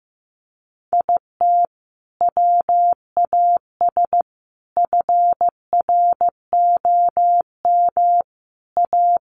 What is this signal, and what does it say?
Morse code, heard as a keyed tone, about 14 wpm, 700 Hz. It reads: IT WAS FROM A
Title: Sentences from Top 1100 Words 15wpm